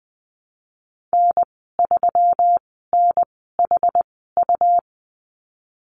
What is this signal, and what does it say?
Morse code, heard as a keyed tone, about 20 wpm, spaced as for 15 wpm, 700 Hz. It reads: N3NHU
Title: 231 Lesson 37g - Letter x - Call Signs 20wpm